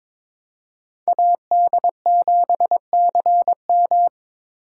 Morse code, heard as a keyed tone, about 26 wpm, 700 Hz. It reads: AD7CM
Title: Speed Racing - USA 2x2 Call Signs 22wpm